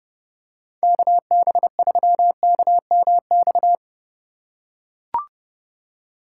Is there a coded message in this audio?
Yes: Morse code from 0.8 to 3.7 s, heard as a keyed tone, about 31 wpm, 700 Hz. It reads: KB3KMX